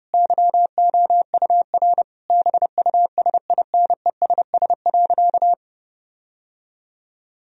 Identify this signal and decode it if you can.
Morse code, heard as a keyed tone, about 32 wpm, 700 Hz. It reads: YOUR BUSINESS.
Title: Sentences from Top 1000 Words - Review Part 4 - 30wpm